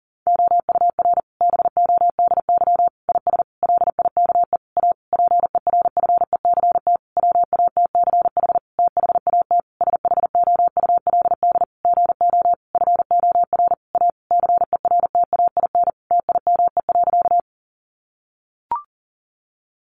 Morse code, heard as a keyed tone, about 37 wpm, 700 Hz. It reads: OUR BODY IS LIKE A PERFECT WATCH THAT SHOULD GO FOR A CERTAIN TIME.
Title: Sentences from Top 1000 Words 40wpm